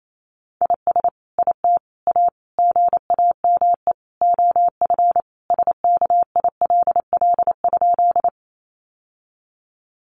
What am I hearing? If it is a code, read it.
Morse code, heard as a keyed tone, about 28 wpm, 700 Hz. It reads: IS IT A GAME OF SKILL?
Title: Sentences from Top 1200 Words 28wpm